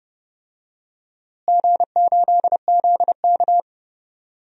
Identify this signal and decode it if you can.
Morse code, heard as a keyed tone, about 30 wpm, 700 Hz. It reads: G8ZK
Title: DXCC-100 Call Signs - 4 Characters - Repeated 2x - 30wpm